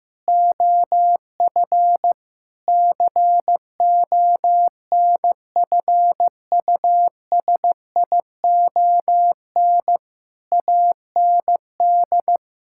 Morse code, heard as a keyed tone, about 15 wpm, 700 Hz. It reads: OF CONFUSION AND
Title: Sentences from Top 1600 Words 15wpm